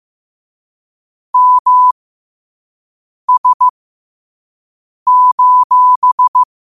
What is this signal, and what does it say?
Morse code, heard as a keyed tone, about 15 wpm, spaced as for 5 wpm, 1000 Hz. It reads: MS:
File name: 5wpm